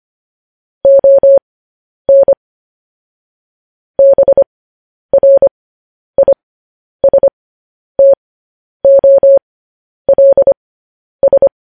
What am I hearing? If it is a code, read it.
Morse code, heard as a keyed tone, about 25 wpm, spaced as for 10 wpm, 550 Hz. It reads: ON BRISTOLS
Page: News Headlines in Morse Code at 10 WPM